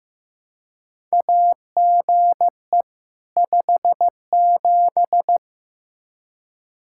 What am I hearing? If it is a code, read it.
Morse code, heard as a keyed tone, about 15 wpm, 700 Hz. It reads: AGE 57
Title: Practice QSO - Exchange 3 - More Info - 15wpm